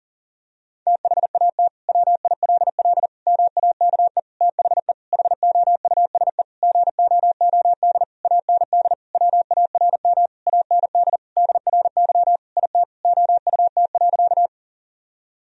Morse code, heard as a keyed tone, about 40 wpm, 700 Hz. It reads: THAT WILL MAKE THE HOUSE GOOD AND WARM AND DRY IT OUT.
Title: Sentences from Top 1000 Words - Review Part 3 - 40wpm